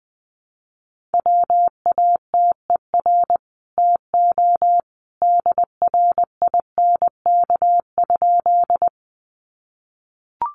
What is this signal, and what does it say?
Morse code, heard as a keyed tone, about 20 wpm, 700 Hz. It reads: WATER TO DRINK?
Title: Sentences from Top 1500 Words 20wpm